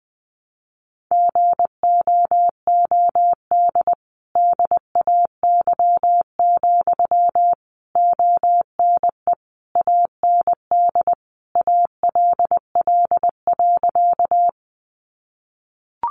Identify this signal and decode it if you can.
Morse code, heard as a keyed tone, about 20 wpm, 700 Hz. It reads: GOOD DAY, ONE AND ALL.
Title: Sentences from Top 200 Words 20wpm